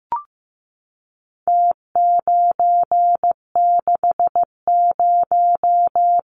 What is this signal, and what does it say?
Morse code, heard as a keyed tone, about 15 wpm, 700 Hz. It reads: T960
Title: Numbers - Contest Style - 4-Digit 15wpm